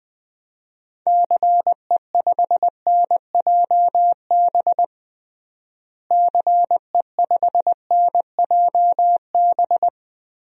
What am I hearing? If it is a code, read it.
Morse code, heard as a keyed tone, about 21 wpm, 700 Hz. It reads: CE5NJB CE5NJB
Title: DXCC-100 Call Signs - 6 Characters - Repeated 3x - 20wpm